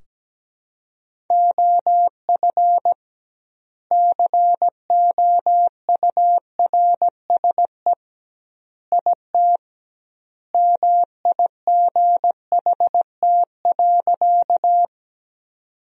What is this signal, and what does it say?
Morse code, heard as a keyed tone, about 17 wpm, 700 Hz. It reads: OF COURSE IT MIGHT.
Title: 2x Word-Spacing - Sentences from Top 500 Words - 17wpm